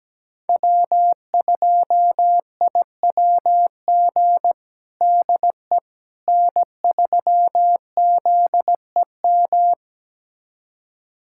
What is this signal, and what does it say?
Morse code, heard as a keyed tone, about 17 wpm, 700 Hz. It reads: W2IWG DE N3ZEM
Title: USA Callsign DE Callsign - 5 Characters - 17wpm